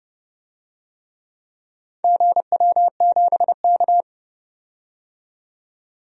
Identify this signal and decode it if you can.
Morse code, heard as a keyed tone, about 30 wpm, 700 Hz. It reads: GW7K
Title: DXCC-100 Call Signs - 4 Characters - 30wpm